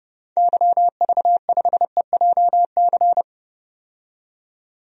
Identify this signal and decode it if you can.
Morse code, heard as a keyed tone, about 30 wpm, 700 Hz. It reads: YV5EJC
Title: DXCC-100 Call Signs - 6 Characters - 30wpm